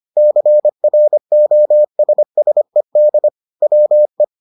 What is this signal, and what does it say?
Morse code, heard as a keyed tone, about 25 wpm, 600 Hz. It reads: CROSSED WE